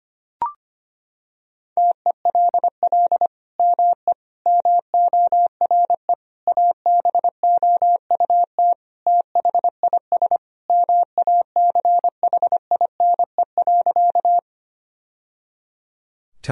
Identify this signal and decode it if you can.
Morse code, heard as a keyed tone, about 25 wpm, 700 Hz. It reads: TELL ME MORE ABOUT THIS MACHINE.
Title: Sentences from Top 1000 Words - Review Part 2 - 25wpm